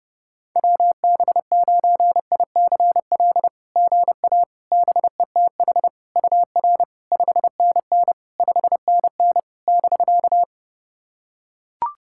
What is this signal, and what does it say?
Morse code, heard as a keyed tone, about 28 wpm, 660 Hz. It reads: WB9ICL GA BETH UR 5NN 5NN <BK>
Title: SOTA QSO Practice 30wpm